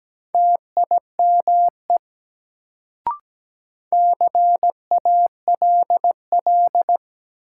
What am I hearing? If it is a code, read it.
Morse code, heard as a keyed tone, about 17 wpm, 700 Hz. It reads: TIME CALL
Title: Sentences from Top 100 Words - New Word Drill - 17wpm